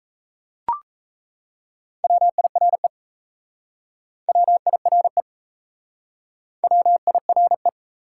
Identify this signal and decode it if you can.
Morse code, heard as a keyed tone, about 35 wpm, 700 Hz. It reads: WIRE WIRE WIRE
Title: Speed-Racing Top 180 QSO Elements 28wpm